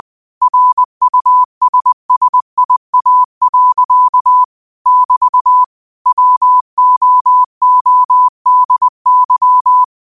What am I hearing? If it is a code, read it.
Morse code, heard as a keyed tone, about 20 wpm, 1000 Hz. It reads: RUSSIA. <BT> WOODY
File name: qotdfiles / 20wpm